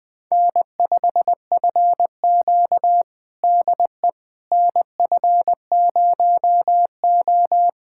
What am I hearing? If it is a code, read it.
Morse code, heard as a keyed tone, about 20 wpm, 700 Hz. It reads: N5FQ DE NF0O
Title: USA Callsign DE Callsign - 4 Characters - 20wpm